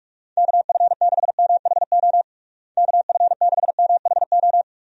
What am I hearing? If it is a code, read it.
Morse code, heard as a keyed tone, about 47 wpm, 700 Hz. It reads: KF6MHO KF6MHO
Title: USA Call Signs - 6 Characters - Repeated 2x - 45wpm